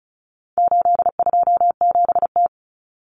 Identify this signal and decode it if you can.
Morse code, heard as a keyed tone, about 34 wpm, 700 Hz. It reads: 827T
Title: Common Cut Numbers - 4-Digit 35wpm